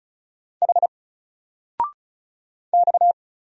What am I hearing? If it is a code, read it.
Morse code, heard as a keyed tone, about 36 wpm, 700 Hz. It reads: H X
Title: Single Letter-Number 35wpm